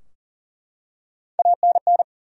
Morse code, heard as a keyed tone, about 40 wpm, 700 Hz. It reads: ANN
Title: USA Common Names - Max 3 Letters - 40wpm